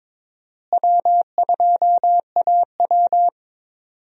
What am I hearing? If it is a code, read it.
Morse code, heard as a keyed tone, about 22 wpm, 700 Hz. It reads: W2AW
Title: USA Call Signs - 4 Characters - Repeated 3x - 22wpm